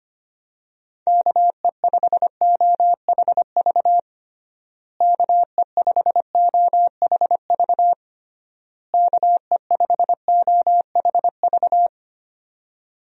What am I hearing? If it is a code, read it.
Morse code, heard as a keyed tone, about 25 wpm, 700 Hz. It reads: KE5OHV KE5OHV KE5OHV
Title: USA Call Signs - 6 Characters - Repeated 3x - 25wpm